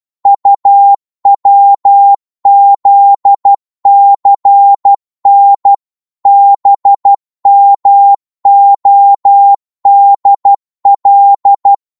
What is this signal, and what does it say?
Morse code, heard as a keyed tone, about 12 wpm, 800 Hz. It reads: UWZCN BMODL
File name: Morse / ltr-12wpm